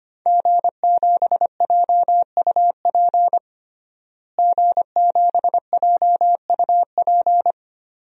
Morse code, heard as a keyed tone, about 25 wpm, 700 Hz. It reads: G7JUP G7JUP
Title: DXCC-100 Call Signs - 5 Characters - Repeated 2x - 25wpm